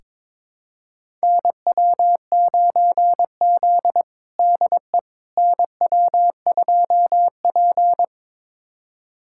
Morse code, heard as a keyed tone, about 22 wpm, 700 Hz. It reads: NW9Z DE NW2P
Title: USA Callsign DE Callsign - 4 Characters - 22wpm